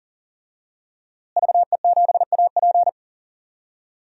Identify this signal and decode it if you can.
Morse code, heard as a keyed tone, about 40 wpm, 700 Hz. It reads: VE7AP
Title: DXCC-100 Call Signs - 5 Characters - Repeated 3x - 40wpm